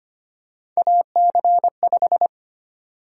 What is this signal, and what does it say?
Morse code, heard as a keyed tone, about 25 wpm, 700 Hz. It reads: AC5
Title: USA Call Sign Prefixes - Repeated 2x - 25wpm